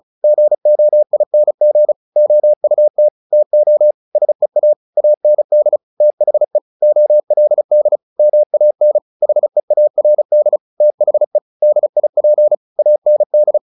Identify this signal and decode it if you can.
Morse code, heard as a keyed tone, about 34 wpm, 600 Hz. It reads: GOING OUT TO SEA AND THE OLD MAN HEARD THE DIP AND
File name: the old man and the sea / the old man and the sea 1 35wpm